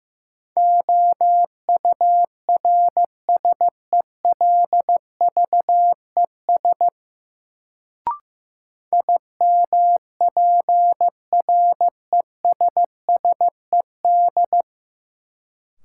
Morse code, heard as a keyed tone, about 15 wpm, 700 Hz. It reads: OURSELVES IMPRESSED
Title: Sentences from Top 1600 Words - New Word Drill - 15wpm